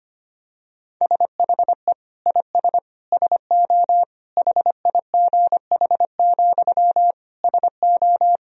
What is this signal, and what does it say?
Morse code, heard as a keyed tone, about 25 wpm, 700 Hz. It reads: SHE IS SO HIGH, SO
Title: Sentences from Top 1000 Words - Review Part 1 - 25wpm